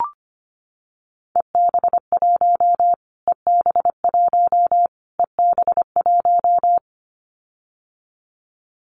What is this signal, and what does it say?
Morse code, heard as a keyed tone, about 25 wpm, 700 Hz. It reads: EB1 EB1 EB1
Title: Callsign Prefixes for Top 25 DXCC - Repeated 3x - 25wpm